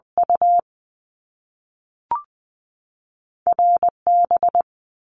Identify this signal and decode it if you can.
Morse code, heard as a keyed tone, about 20 wpm, 700 Hz. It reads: U RB
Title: Top Two Letter Combinations in Words 20wpm